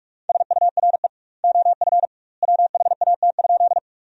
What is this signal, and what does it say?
Morse code, heard as a keyed tone, about 46 wpm, 700 Hz. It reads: SURE OF WHAT?
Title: 2x Word-Spacing - Sentences from Top 300 Words - 45wpm